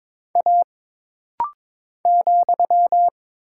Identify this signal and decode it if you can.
Morse code, heard as a keyed tone, about 22 wpm, 700 Hz. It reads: A ,
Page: This rapid-fire timing is geared to push your ICR (Instant Character Recognition) skills.